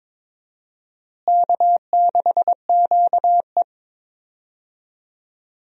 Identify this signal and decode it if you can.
Morse code, heard as a keyed tone, about 22 wpm, 700 Hz. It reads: K6QE
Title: USA Call Signs - 4 Characters - Repeated 3x - 22wpm